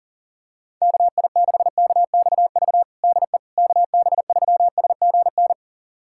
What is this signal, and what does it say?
Morse code, heard as a keyed tone, about 40 wpm, 700 Hz. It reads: KI6KXV DE KB3SGN
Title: USA Callsign DE Callsign 40wpm